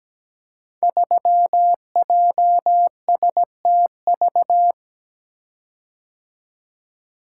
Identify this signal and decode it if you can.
Morse code, heard as a keyed tone, about 17 wpm, 700 Hz. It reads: 3JSTV
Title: Code Groups - 5 Letter:Numbers - 17wpm